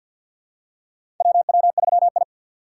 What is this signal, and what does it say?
Morse code, heard as a keyed tone, about 46 wpm, 700 Hz. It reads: WW3I
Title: USA Call Signs - 3x Character Spacing - 50wpm